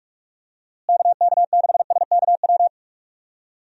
Morse code, heard as a keyed tone, about 46 wpm, 700 Hz. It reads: KK6SKW
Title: USA Call Signs - 6 Characters - Repeated 3x - 45wpm